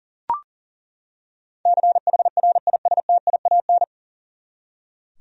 Morse code, heard as a keyed tone, about 40 wpm, 700 Hz. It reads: CHRISTIAN
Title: USA Common Names - 40wpm